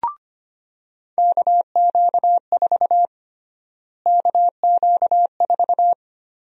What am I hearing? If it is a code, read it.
Morse code, heard as a keyed tone, about 25 wpm, 700 Hz. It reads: KQ4 KQ4
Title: USA Call Sign Prefixes - Repeated 2x - 25wpm